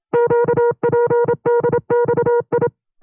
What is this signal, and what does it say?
Morse code, heard as a keyed tone, about 27 wpm, 480 Hz. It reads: QPDXI
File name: FreeFiles / CW-practice / 25WPM-side1